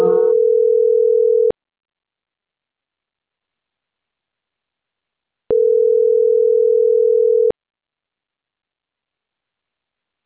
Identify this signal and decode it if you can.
Ringback tone (North America), 440 + 480 Hz.